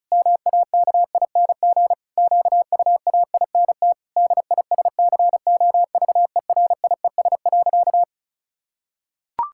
Morse code, heard as a keyed tone, about 36 wpm, 700 Hz. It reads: MAKING QUAINT DISCOVERIES.
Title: Pangram Challenge v2 - 35wpm